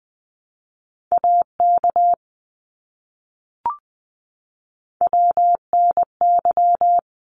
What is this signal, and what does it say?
Morse code, heard as a keyed tone, about 20 wpm, 700 Hz. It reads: AK WNY
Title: ARRL Sections 20wpm